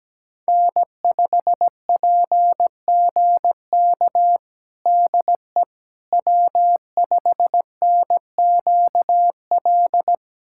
Morse code, heard as a keyed tone, about 17 wpm, 700 Hz. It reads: N5PGK DE W5NQL
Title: USA Callsign DE Callsign - 5 Characters - 17wpm